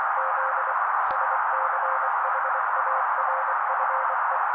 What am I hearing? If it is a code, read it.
Morse code, heard as a keyed tone, about 24 wpm, 530 Hz, with noise. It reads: Z SCHARFE
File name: daily / 23wpm noise